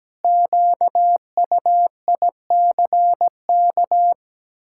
Morse code, heard as a keyed tone, about 17 wpm, 700 Hz. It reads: QUICK
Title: Pangram Challenge v2 - 17wpm